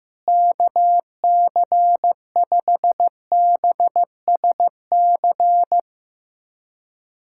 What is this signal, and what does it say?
Morse code, heard as a keyed tone, about 15 wpm, 700 Hz. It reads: KC5BSC
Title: USA Callsign DE Callsign - 6 Characters - 15wpm